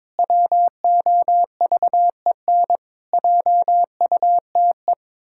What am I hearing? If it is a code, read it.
Morse code, heard as a keyed tone, about 22 wpm, 700 Hz. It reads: WOVEN JUTE
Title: Pangram Challenge v2 - 22wpm